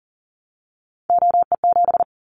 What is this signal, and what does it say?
Morse code, heard as a keyed tone, about 40 wpm, 700 Hz. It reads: OE7
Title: Callsign Prefixes for Top 25 DXCC - Repeated 3x - 40wpm